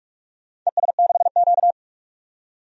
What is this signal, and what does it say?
Morse code, heard as a keyed tone, about 44 wpm, 700 Hz. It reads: ES6Q
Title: DXCC-100 Call Signs - 4 Characters - 45wpm